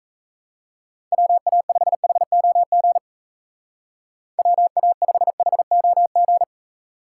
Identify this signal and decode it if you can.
Morse code, heard as a keyed tone, about 35 wpm, 700 Hz. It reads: WA5HOG WA5HOG
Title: Speed Racing - USA 2x3 Call Signs 28wpm